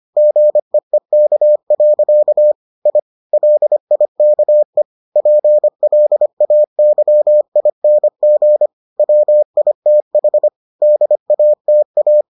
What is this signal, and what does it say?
Morse code, heard as a keyed tone, about 25 wpm, 600 Hz. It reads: GEEK. I LIKE PLAYING WITH DATA